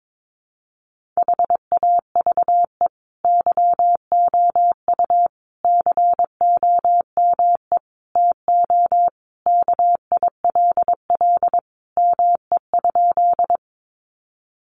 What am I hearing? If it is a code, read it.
Morse code, heard as a keyed tone, about 22 wpm, 700 Hz. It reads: HAVE YOU COME TO KILL ME?
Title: Sentences from Top 1400 Words 22wpm